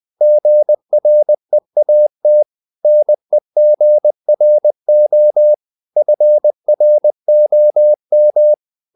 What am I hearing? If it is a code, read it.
Morse code, heard as a keyed tone, about 20 wpm, 600 Hz. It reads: GREAT NEGRO FROM
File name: the old man and the sea / the old man and the sea part 1 20wpm